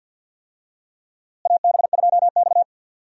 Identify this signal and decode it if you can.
Morse code, heard as a keyed tone, about 46 wpm, 700 Hz. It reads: AB2X
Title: USA Call Signs - 4 Characters - Repeated 2x - 50wpm